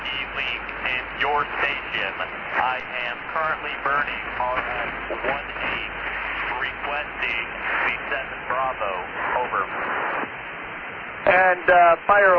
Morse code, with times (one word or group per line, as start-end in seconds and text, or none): none